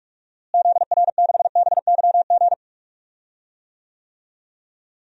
Morse code, heard as a keyed tone, about 46 wpm, 700 Hz. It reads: ZR6BYG
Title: DXCC-100 Call Signs - 6 Characters - Repeated 2x - 45wpm